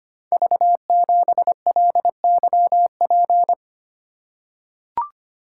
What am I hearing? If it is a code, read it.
Morse code, heard as a keyed tone, about 25 wpm, 700 Hz. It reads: V7LYP
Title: Code Groups - 5 Letter:Numbers - 25wpm